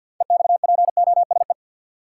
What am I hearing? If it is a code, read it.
Morse code, heard as a keyed tone, about 52 wpm, 700 Hz. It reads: EXPOSE
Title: Six Letter Words - 50wpm